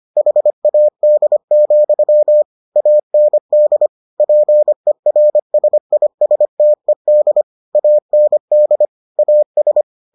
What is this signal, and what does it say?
Morse code, heard as a keyed tone, about 25 wpm, 600 Hz. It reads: HAD, AND PERSISTED AND AS